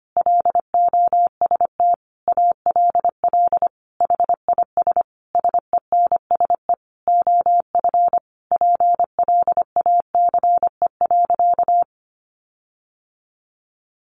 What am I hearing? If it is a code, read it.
Morse code, heard as a keyed tone, about 25 wpm, 700 Hz. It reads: LOST ALL HIS SENSE OF PLACE.
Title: Sentences from Top 1400 Words 25wpm